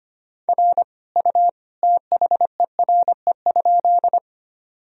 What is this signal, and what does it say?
Morse code, heard as a keyed tone, about 25 wpm, 700 Hz. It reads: R U THERE?
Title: Sentences from Top 100 Words and Abbreviations 25wpm